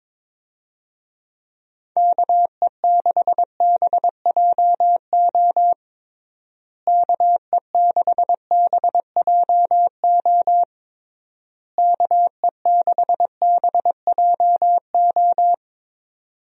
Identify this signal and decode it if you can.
Morse code, heard as a keyed tone, about 21 wpm, 700 Hz. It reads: KE6BJO KE6BJO KE6BJO
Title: USA Call Signs - 6 Characters - Repeated 3x - 22wpm